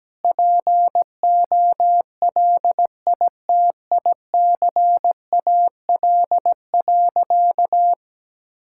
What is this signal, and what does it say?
Morse code, heard as a keyed tone, about 17 wpm, 700 Hz. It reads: POLITICAL.
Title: Sentences from Top 1000 Words - Review Part 5 - 17wpm